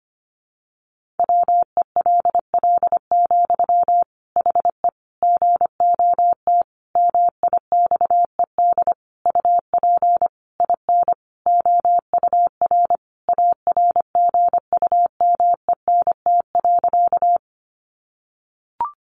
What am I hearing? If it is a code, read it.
Morse code, heard as a keyed tone, about 25 wpm, 700 Hz. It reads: WELL, HE GOT MIXED UP IN OUR ARGUMENT.
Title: Sentences from Top 1500 Words 25wpm